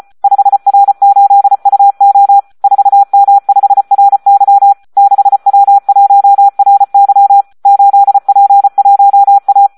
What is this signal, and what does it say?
Morse code, heard as a keyed tone, about 34 wpm, 800 Hz. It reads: 5R8UO 4M5RY 6W1RY 8P1A
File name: Audio practica CW / SD 1 34